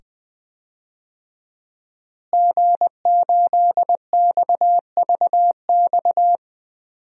Morse code, heard as a keyed tone, about 20 wpm, 700 Hz. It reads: G8XVX
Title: DXCC-100 Call Signs - 5 Characters - Repeated 2x - 20wpm